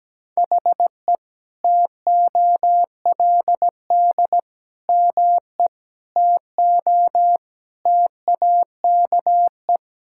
Morse code, heard as a keyed tone, about 17 wpm, 700 Hz. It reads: HE TOLD ME TO TAKE
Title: Sentences from Top 1000 Words - Review Part 6 - 17wpm